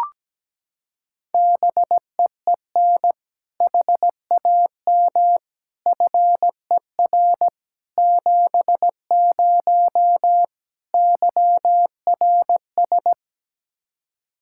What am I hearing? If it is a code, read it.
Morse code, heard as a keyed tone, about 17 wpm, 700 Hz. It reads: BEEN HAM FER 70 YRS
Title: Practice QSO - Exchange 3 - More Info - 17wpm